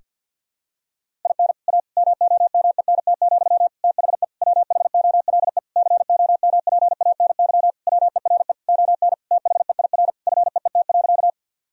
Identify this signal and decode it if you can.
Morse code, heard as a keyed tone, about 52 wpm, 700 Hz. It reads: IN A MOMENT, THE WHOLE COMPANY WERE ON THEIR FEET.
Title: Sentences from Top 1000 Words - Review Part 3 - 50wpm